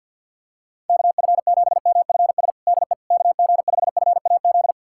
Morse code, heard as a keyed tone, about 51 wpm, 700 Hz. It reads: KF7MFS DE KG5FAZ
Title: USA Callsign DE Callsign - 6 Characters - 50wpm